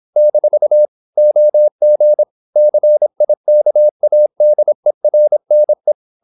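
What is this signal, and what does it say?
Morse code, heard as a keyed tone, about 26 wpm, 600 Hz. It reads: - OG CIKADERNE
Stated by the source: High-speed (QRQ) Morse at 26 wpm.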